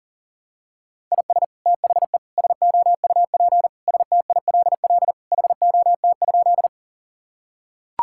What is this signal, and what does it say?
Morse code, heard as a keyed tone, about 40 wpm, 700 Hz. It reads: IS THE SOUP STILL HOT?
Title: Sentences from Top 1000 Words - Review Part 6 - 40wpm